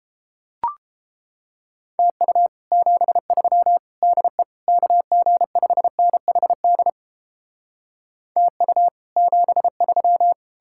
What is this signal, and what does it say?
Morse code, heard as a keyed tone, about 33 wpm, 700 Hz. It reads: TU 73 DE KG5NHD TU 73